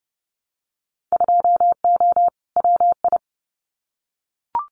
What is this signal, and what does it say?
Morse code, heard as a keyed tone, about 30 wpm, 700 Hz. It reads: 2O WI
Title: Winter Field Day Exchanges 30wpm